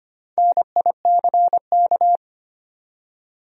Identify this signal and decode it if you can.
Morse code, heard as a keyed tone, about 25 wpm, 700 Hz. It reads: NICK
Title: USA Common Names - 25wpm